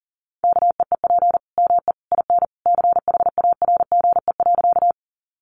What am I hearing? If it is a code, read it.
Morse code, heard as a keyed tone, about 40 wpm, 700 Hz. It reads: KEEP ME IN CHARGE.
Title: Sentences from Top 900 Words 40wpm